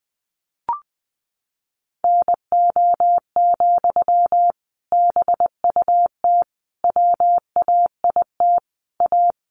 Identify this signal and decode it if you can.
Morse code, heard as a keyed tone, about 20 wpm, 700 Hz. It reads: NO, BUT WAIT A
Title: Sentences from Top 900 Words 20wpm